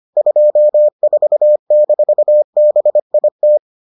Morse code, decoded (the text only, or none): 24-BIT